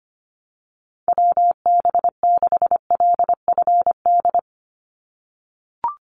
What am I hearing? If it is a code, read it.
Morse code, heard as a keyed tone, about 25 wpm, 700 Hz. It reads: WB6LFD
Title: USA Call Signs 25wpm